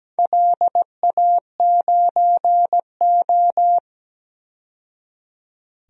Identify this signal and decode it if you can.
Morse code, heard as a keyed tone, about 17 wpm, 700 Hz. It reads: LA9O
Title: DXCC-100 Call Signs - 4 Characters - 17wpm